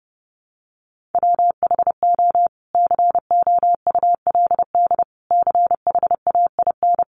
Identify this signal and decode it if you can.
Morse code, heard as a keyed tone, about 30 wpm, 700 Hz. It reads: WHO COULD CHAIN
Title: Sentences from Top 1500 Words 30wpm